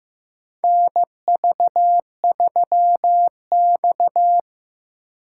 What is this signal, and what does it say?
Morse code, heard as a keyed tone, about 15 wpm, 700 Hz. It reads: NV3X
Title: USA Call Signs - 4 Characters - 4x Character Spacing - 15wpm